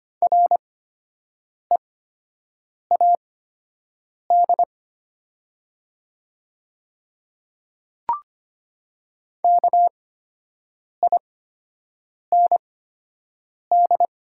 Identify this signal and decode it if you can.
Morse code, heard as a keyed tone, about 25 wpm, spaced as for 7 wpm, 700 Hz. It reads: READ KIND
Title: 8x Character Spacing - Top 200 Words - 25wpm